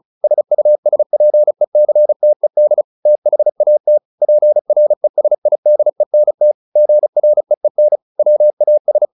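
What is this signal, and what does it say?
Morse code, heard as a keyed tone, about 36 wpm, 600 Hz. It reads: SUSPECTED THAT PRESIDENT GREEN WAS
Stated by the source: Morse practice at 35 wpm from CW med Gnister, sending Ken Follett's novel Never in English.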